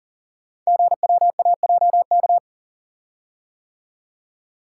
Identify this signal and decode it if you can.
Morse code, heard as a keyed tone, about 40 wpm, 700 Hz. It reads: GWAJK
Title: Code Groups - 5 Letter:Numbers - 40wpm